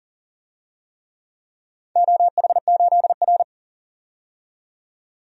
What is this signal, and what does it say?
Morse code, heard as a keyed tone, about 41 wpm, 700 Hz. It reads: OH8R
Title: DXCC-100 Call Signs - 4 Characters - 40wpm